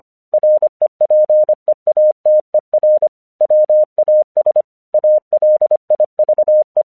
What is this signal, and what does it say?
Morse code, heard as a keyed tone, about 25 wpm, 610 Hz. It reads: REPEATER WAS ALIVE
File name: 1 English 25wpm / AQC05